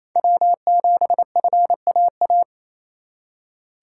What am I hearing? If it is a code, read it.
Morse code, heard as a keyed tone, about 28 wpm, 700 Hz. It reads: W7FAA
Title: USA Call Signs - 5 Characters - 8x Character Spacing - 28wpm